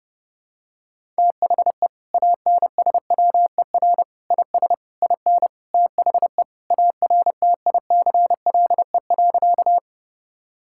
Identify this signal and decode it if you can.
Morse code, heard as a keyed tone, about 29 wpm, 700 Hz. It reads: THE ANSWER IS IN THE ARTICLE.
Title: Sentences from Top 1000 Words - Review Part 5 - 30wpm